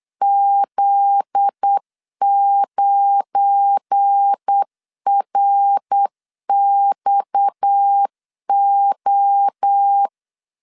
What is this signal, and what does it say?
Morse code, heard as a keyed tone, about 8 wpm, 800 Hz. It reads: Z9RXO
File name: LEZIONI / 7 GF9Z / 7 GF9Z 8wpm 3213